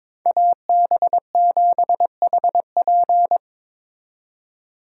Morse code, heard as a keyed tone, about 25 wpm, 700 Hz. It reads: AB7HP
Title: Speed Racing - USA 2x2 Call Signs 22wpm